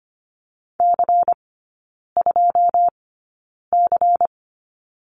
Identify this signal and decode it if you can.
Morse code, heard as a keyed tone, about 25 wpm, spaced as for 9 wpm, 700 Hz. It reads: C2C
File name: course / 25:10wpm / 093 Lesson 16e - Letter c - Three Characters 25wpm